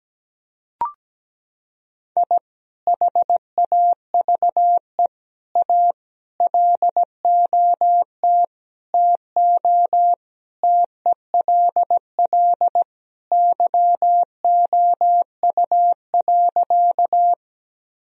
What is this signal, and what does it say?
Morse code, heard as a keyed tone, about 17 wpm, 700 Hz. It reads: I HAVE A LOT TO TELL YOU.
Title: Sentences from Top 1000 Words - Review Part 3 - 17wpm